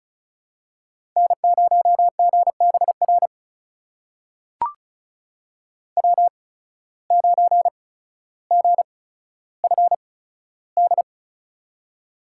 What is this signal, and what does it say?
Morse code, heard as a keyed tone, about 36 wpm, spaced as for 15 wpm, 700 Hz. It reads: N0GBR W9GFD
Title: USA Call Signs - 8x Character Spacing - 35wpm